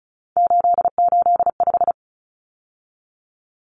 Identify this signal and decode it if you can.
Morse code, heard as a keyed tone, about 34 wpm, 700 Hz. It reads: 885
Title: Numbers - 3-Digit 35wpm